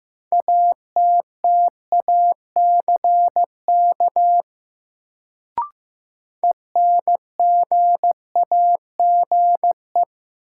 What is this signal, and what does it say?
Morse code, heard as a keyed tone, about 15 wpm, 700 Hz. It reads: ATTACK ENGAGE
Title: Sentences from Top 1500 Words - New Word Drill - 15wpm